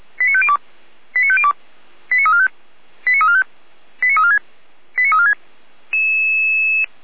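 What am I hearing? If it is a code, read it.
Five-tone selective call (ZVEI), heard as quick runs of five tones, tones of about 80 ms. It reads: 89502 89502 89345 89345 89346 89346
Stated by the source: ZVEI five-tone selective call received on an SDR.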